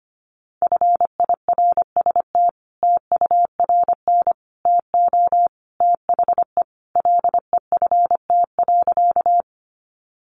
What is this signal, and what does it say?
Morse code, heard as a keyed tone, about 25 wpm, 700 Hz. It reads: FIRST TURN TO THE LEFT.
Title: Sentences from Top 300 Words 25wpm